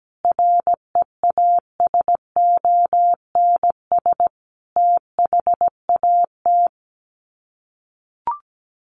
Morse code, heard as a keyed tone, about 17 wpm, 700 Hz. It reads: REASONS THAT
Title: Sets of 5 Words Encore 17wpm